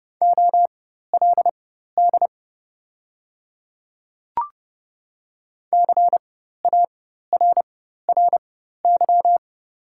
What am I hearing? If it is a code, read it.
Morse code, heard as a keyed tone, about 30 wpm, spaced as for 14 wpm, 700 Hz. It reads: OLD CARRY
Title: 4x Character Spacing - Top 300 Words - 30wpm